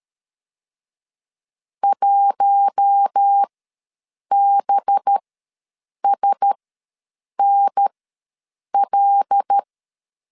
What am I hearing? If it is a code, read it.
Morse code, heard as a keyed tone, about 13 wpm, spaced as for 7 wpm, 800 Hz. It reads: 1BSNL
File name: LEZIONI / 10 SIGNS / 10C DOT ws 12wpm 3548